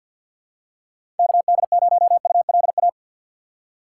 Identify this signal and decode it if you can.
Morse code, heard as a keyed tone, about 53 wpm, 700 Hz. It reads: KD0ULA
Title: USA Call Signs - 6 Characters - 8x Character Spacing - 50wpm